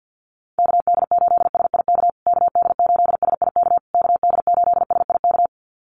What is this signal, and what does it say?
Morse code, heard as a keyed tone, about 50 wpm, 700 Hz. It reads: KD8SIK KD8SIK KD8SIK
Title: USA Call Signs Repeated 3x 50wpm